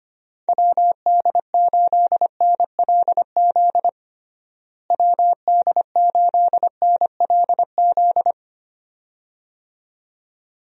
Morse code, heard as a keyed tone, about 25 wpm, 700 Hz. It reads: WD8NLZ WD8NLZ
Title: USA Call Signs - 6 Characters - Repeated 2x - 25wpm